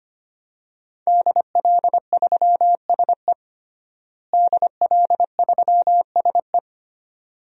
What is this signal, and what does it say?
Morse code, heard as a keyed tone, about 25 wpm, 700 Hz. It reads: DL3SE DL3SE
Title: DXCC-100 Call Signs - 5 Characters - Repeated 2x - 25wpm